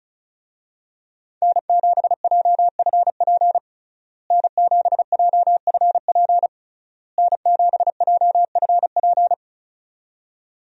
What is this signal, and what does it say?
Morse code, heard as a keyed tone, about 34 wpm, 700 Hz. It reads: N7JFP N7JFP N7JFP
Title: USA Call Signs - 5 Characters - Repeated 3x - 35wpm